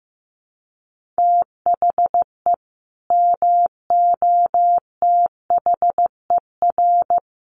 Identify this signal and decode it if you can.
Morse code, heard as a keyed tone, about 14 wpm, 700 Hz. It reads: THE MOTHER
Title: Sentences from Top 1500 Words 15wpm